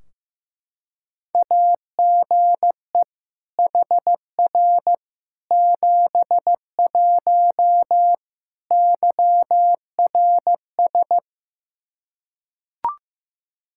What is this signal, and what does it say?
Morse code, heard as a keyed tone, about 15 wpm, 700 Hz. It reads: AGE HR 71 YRS
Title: Practice QSO - Exchange 3 - More Info - 15wpm